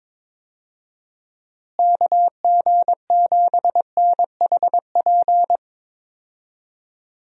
Morse code, heard as a keyed tone, about 22 wpm, 700 Hz. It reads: KG7NHP
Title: USA Call Signs - 6 Characters - Repeated 3x - 22wpm